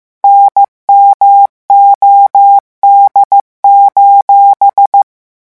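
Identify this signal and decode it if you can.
Morse code, heard as a keyed tone, about 15 wpm, 800 Hz. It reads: NMOD:
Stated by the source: Morse code revision lesson 16 at 14 words per minute.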